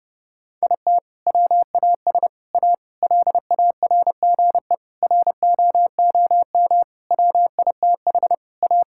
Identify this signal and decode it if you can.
Morse code, heard as a keyed tone, about 30 wpm, 700 Hz. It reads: IT WAS A LARGE ROOM WITH A
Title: Sentences from Top 1600 Words 30wpm